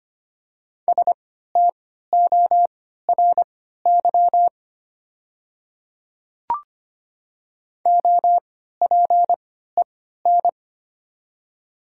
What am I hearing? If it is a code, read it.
Morse code, heard as a keyed tone, about 25 wpm, spaced as for 14 wpm, 700 Hz. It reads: STORY OPEN
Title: 3x Character Spacing - Top 300 Words - 25wpm